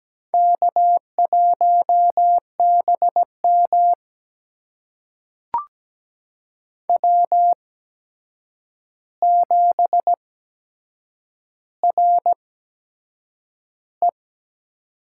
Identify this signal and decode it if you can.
Morse code, heard as a keyed tone, about 17 wpm, spaced as for 7 wpm, 700 Hz. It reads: K1BM W7RE
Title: USA Call Signs - 4 Characters - 8x Character Spacing - 17wpm